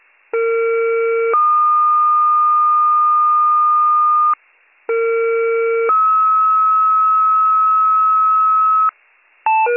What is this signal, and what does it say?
Two-tone page, heard as a one-second tone, then a three-second tone, about 460 then 1220 Hz, about 460 then 1290 Hz. It is heard from 0.3 to 8.9 s.